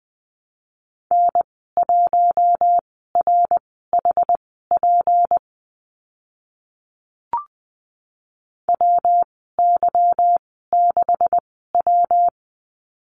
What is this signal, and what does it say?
Morse code, heard as a keyed tone, about 20 wpm, spaced as for 15 wpm, 700 Hz. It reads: N1RHP WY6W
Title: 167 Lesson 27g - Number 7 - Call Signs 20wpm